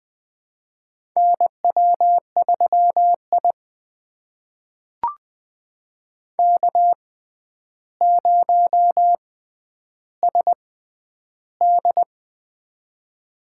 Morse code, heard as a keyed tone, about 20 wpm, spaced as for 10 wpm, 700 Hz. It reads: NW3I K0SD
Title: USA Call Signs - 4 Characters - 6x Character Spacing - 20wpm